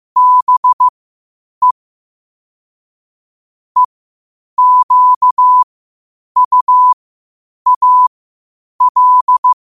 Morse code, heard as a keyed tone, about 15 wpm, spaced as for 8 wpm, 1000 Hz. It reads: BE EQUAL
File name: qotdfiles / 8wpm